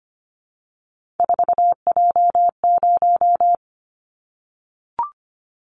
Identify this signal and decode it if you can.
Morse code, heard as a keyed tone, about 25 wpm, 700 Hz. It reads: 4J0
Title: Callsign Prefixes for Top 100 DXCC - 25wpm